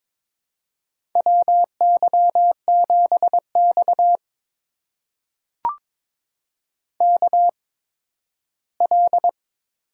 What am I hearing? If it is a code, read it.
Morse code, heard as a keyed tone, about 22 wpm, spaced as for 13 wpm, 700 Hz. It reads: WY7X KL
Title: USA Call Signs - 4 Characters - 8x Character Spacing - 22wpm